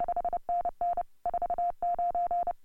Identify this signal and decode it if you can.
Morse code, heard as a keyed tone, about 28 wpm, 700 Hz. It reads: 5NN 49
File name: AUDIO2018 / HA3FMR CW